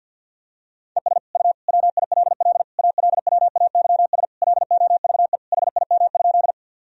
Morse code, heard as a keyed tone, about 46 wpm, 700 Hz. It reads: ES U WILL ALWAYS LOVE HIM?
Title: Sentences from Top 400 Words and Abbreviations 50wpm